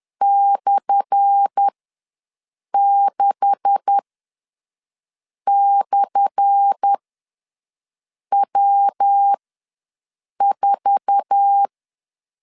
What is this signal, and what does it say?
Morse code, heard as a keyed tone, about 10 wpm, spaced as for 5 wpm, 800 Hz. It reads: /6/W4
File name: LEZIONI / 10 SIGNS / 10A SLASH ws 10wpm 4121